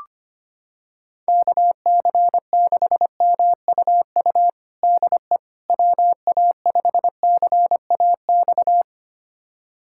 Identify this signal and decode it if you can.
Morse code, heard as a keyed tone, about 25 wpm, 700 Hz. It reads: KC6MUU DE WA5CAX
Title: USA Callsign DE Callsign - 6 Characters - 25wpm